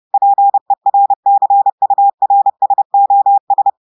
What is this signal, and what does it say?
Morse code, heard as a keyed tone, about 30 wpm, 800 Hz. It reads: PERCURSOS
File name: audios / portugais / portugais texte en clair 1 vitesse 30 mots minutes